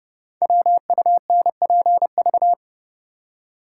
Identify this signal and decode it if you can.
Morse code, heard as a keyed tone, about 29 wpm, 700 Hz. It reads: WUNPV
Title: Code Groups - 5 Letters - 30wpm